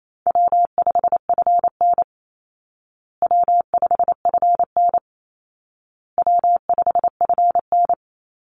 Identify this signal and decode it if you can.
Morse code, heard as a keyed tone, about 28 wpm, 700 Hz. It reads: W5FN W5FN W5FN
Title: Call Signs Repeated 2x - 28wpm